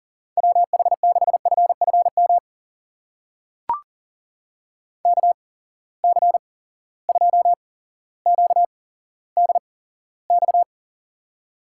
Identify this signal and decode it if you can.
Morse code, heard as a keyed tone, about 40 wpm, spaced as for 17 wpm, 700 Hz. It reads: WH6FFM KC2QDX
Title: USA Call Signs - 6 Characters - 8x Character Spacing - 40wpm